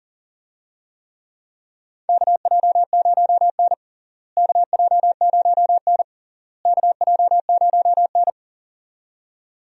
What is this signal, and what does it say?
Morse code, heard as a keyed tone, about 40 wpm, 700 Hz. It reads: KJ0N KJ0N KJ0N
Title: USA Call Signs - 4 Characters - Repeated 3x - 40wpm